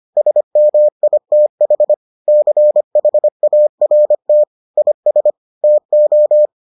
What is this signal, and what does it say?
Morse code, heard as a keyed tone, about 25 wpm, 600 Hz. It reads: SMITH CHART IS TO